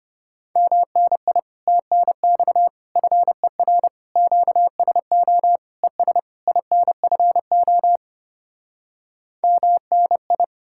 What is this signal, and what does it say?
Morse code, heard as a keyed tone, about 33 wpm, 700 Hz. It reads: MNI TNX FER QSO ES INFO MNI